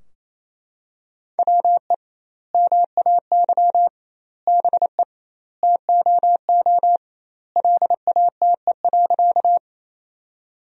Morse code, heard as a keyed tone, about 27 wpm, 700 Hz. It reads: WE MAY BE TOO LATE.
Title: 2x Word-Spacing - Sentences from Top 300 Words - 28wpm